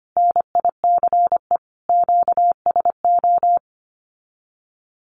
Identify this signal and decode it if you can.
Morse code, heard as a keyed tone, about 25 wpm, 700 Hz. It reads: NICE QSO
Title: Common QSO Phrases Encore 25wpm